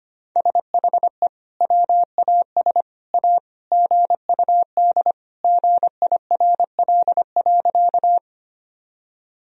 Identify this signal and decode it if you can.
Morse code, heard as a keyed tone, about 25 wpm, 700 Hz. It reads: SHE WAS A GUD GIRL.
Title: Sentences from Top 400 Words and Abbreviations 25wpm